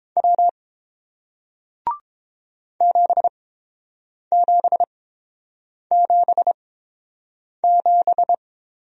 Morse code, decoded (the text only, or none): W 7777